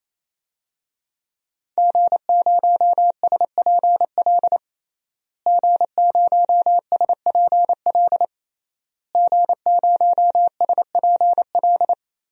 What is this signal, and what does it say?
Morse code, heard as a keyed tone, about 28 wpm, 700 Hz. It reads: G0SPL G0SPL G0SPL
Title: DXCC-100 Call Signs - 5 Characters - Repeated 3x - 28wpm